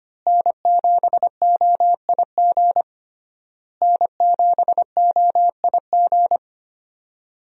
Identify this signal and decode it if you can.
Morse code, heard as a keyed tone, about 25 wpm, 700 Hz. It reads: N7OIG N7OIG
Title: USA Call Signs - 5 Characters - Repeated 3x - 25wpm